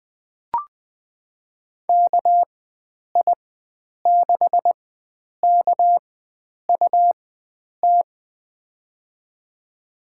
Morse code, heard as a keyed tone, about 19 wpm, spaced as for 9 wpm, 700 Hz. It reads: KI6KUT
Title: USA Call Signs - 6 Characters - 4x Character Spacing - 20wpm